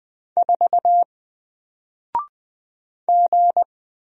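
Morse code, heard as a keyed tone, about 20 wpm, 700 Hz. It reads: 4 G
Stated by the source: Single letter and number Morse drill at 20 wpm, from N4LCD.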